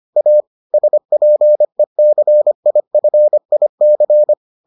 Morse code, heard as a keyed tone, about 25 wpm, 600 Hz. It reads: A SPECIFIC